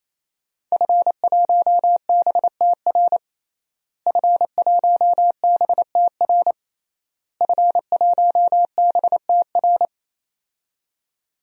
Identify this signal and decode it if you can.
Morse code, heard as a keyed tone, about 28 wpm, 700 Hz. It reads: F1BTR F1BTR F1BTR
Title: DXCC-100 Call Signs - 5 Characters - Repeated 3x - 28wpm